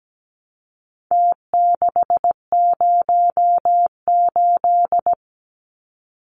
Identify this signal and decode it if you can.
Morse code, heard as a keyed tone, about 17 wpm, 700 Hz. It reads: T608
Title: Numbers - Contest Style - 4-Digit 17wpm